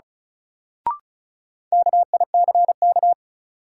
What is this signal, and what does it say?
Morse code, heard as a keyed tone, about 36 wpm, 700 Hz. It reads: KICK
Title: Sentences from Top 1300 Words - New Word Drill - 35wpm